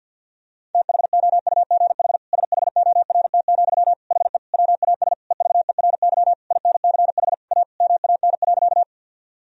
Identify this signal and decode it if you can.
Morse code, heard as a keyed tone, about 50 wpm, 700 Hz. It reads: THOUGH SHORT, HE WAS EVERY INCH A MAN.
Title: Sentences from Top 1000 Words - Review Part 3 - 50wpm